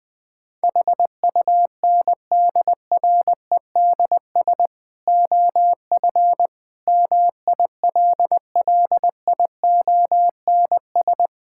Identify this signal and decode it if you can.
Morse code, heard as a keyed tone, about 21 wpm, 700 Hz. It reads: HUNDREDS OF MILLIONS
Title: Build A Phrase - 5-Word Phrases - 20wpm